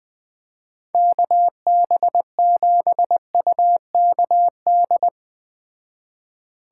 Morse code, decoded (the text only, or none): KB7UKD